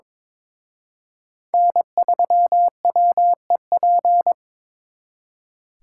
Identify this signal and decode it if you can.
Morse code, heard as a keyed tone, about 26 wpm, 700 Hz. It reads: N3WEP
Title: Speed Racing - USA Call Signs 22wpm